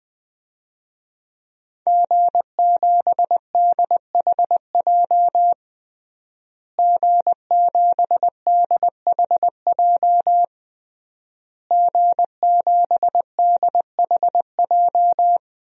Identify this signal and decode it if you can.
Morse code, heard as a keyed tone, about 21 wpm, 700 Hz. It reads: G7DHJ G7DHJ G7DHJ
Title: DXCC-100 Call Signs - 5 Characters - Repeated 3x - 20wpm